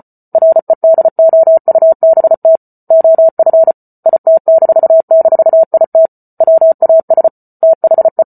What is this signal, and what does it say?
Morse code, heard as a keyed tone, about 34 wpm, 650 Hz. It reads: REDOUBT OF IT--IT WAS THE